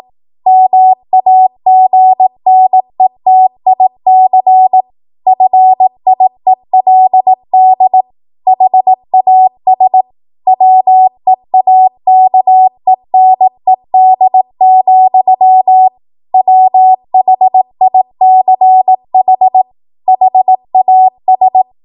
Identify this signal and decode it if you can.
Morse code, heard as a keyed tone, about 18 wpm, 750 Hz. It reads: MAGNETIC FIELD HAS WEAKENED, WHICH HAS
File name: Morse / 18WPM